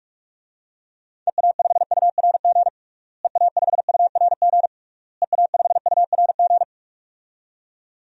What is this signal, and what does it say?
Morse code, heard as a keyed tone, about 46 wpm, 700 Hz. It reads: EA5URG EA5URG EA5URG
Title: DXCC-100 Call Signs - 6 Characters - Repeated 3x - 45wpm